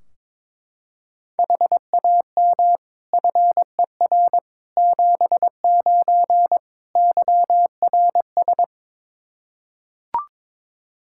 Morse code, heard as a keyed tone, about 22 wpm, 700 Hz. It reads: HAM FER 79 YRS
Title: Practice QSO - Exchange 3 - More Info - 22wpm